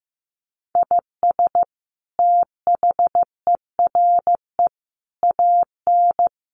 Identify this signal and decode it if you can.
Morse code, heard as a keyed tone, about 15 wpm, 700 Hz. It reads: IS THERE AN
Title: Sentences from Top 1300 Words 15wpm